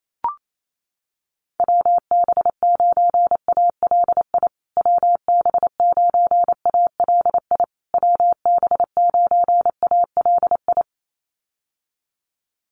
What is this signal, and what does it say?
Morse code, heard as a keyed tone, about 28 wpm, 700 Hz. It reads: WB9ALI WB9ALI WB9ALI
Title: USA Call Signs Repeated 3x 28wpm